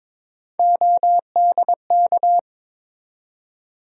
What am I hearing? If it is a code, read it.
Morse code, heard as a keyed tone, about 21 wpm, 700 Hz. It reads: ODK
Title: USA Call Sign Postfixes - 3 Letters 22wpm